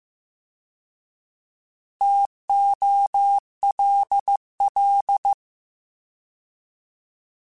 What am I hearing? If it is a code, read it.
Morse code, heard as a keyed tone, about 15 wpm, 800 Hz. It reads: TOLL